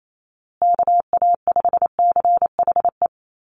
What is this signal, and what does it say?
Morse code, heard as a keyed tone, about 28 wpm, 700 Hz. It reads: KA5CHE
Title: USA Call Signs Repeated 3x 28wpm